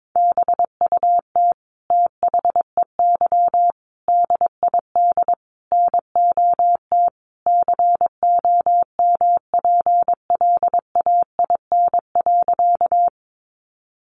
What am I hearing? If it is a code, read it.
Morse code, heard as a keyed tone, about 22 wpm, 700 Hz. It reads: BUT THEY DID NOT COMPLAIN.
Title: Sentences from Top 1300 Words 22wpm